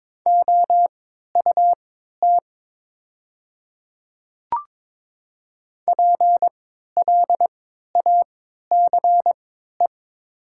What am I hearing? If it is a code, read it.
Morse code, heard as a keyed tone, about 22 wpm, spaced as for 13 wpm, 700 Hz. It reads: OUT PLACE
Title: 3x Character Spacing - Top 200 Words - 22wpm